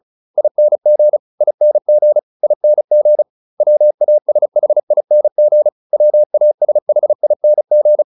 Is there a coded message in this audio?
Yes: Morse code from 0.4 to 8.0 s, heard as a keyed tone, about 34 wpm, 600 Hz. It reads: ING ING ING WASHING WASHING